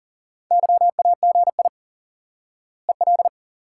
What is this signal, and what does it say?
Morse code, heard as a keyed tone, about 40 wpm, 700 Hz. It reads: YAGI EL
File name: IQE Course 3 - Lesson 89 - YAGI - Two Elements - 40wpm